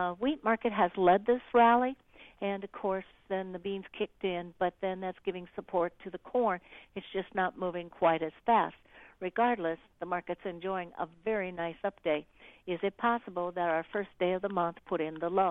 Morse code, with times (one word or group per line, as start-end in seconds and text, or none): none